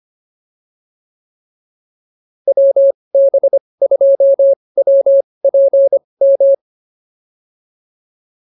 Morse code, heard as a keyed tone, about 25 wpm, spaced as for 20 wpm, 550 Hz. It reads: WB2WPM
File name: POTA210a 20